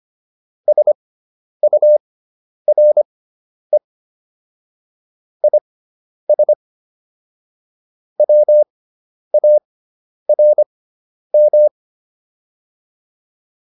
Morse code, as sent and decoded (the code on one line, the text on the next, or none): ... ..- .-. / .. ... / .-- .- .-. --
SURE IS WARM